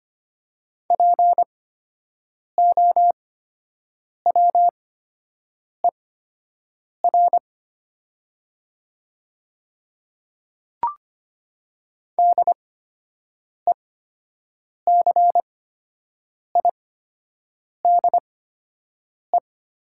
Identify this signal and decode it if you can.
Morse code, heard as a keyed tone, about 25 wpm, spaced as for 7 wpm, 700 Hz. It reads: POWER DECIDE
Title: 8x Character Spacing - Top 500 Words - 25wpm